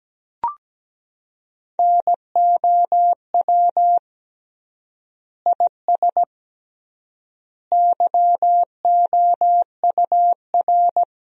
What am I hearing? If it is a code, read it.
Morse code, heard as a keyed tone, about 17 wpm, 700 Hz. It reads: NOW IS YOUR
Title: 3x Word-Spacing - Sentences from Top 100 Words - 17wpm